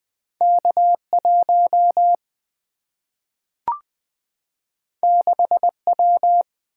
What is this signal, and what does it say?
Morse code, heard as a keyed tone, about 20 wpm, 700 Hz. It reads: K1 6W
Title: Single-Letter Single-Number 20wpm